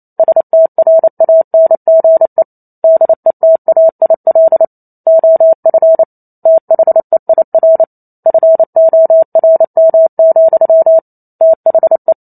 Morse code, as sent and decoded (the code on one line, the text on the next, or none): ... - .-. .- -. --. / -.. - .- .. .-.. / --- ..-. / - .... .. .-. / ..-. --- .-. -- --..-- / - .... .
STRANGE DETAIL OF THEIR FORM, THE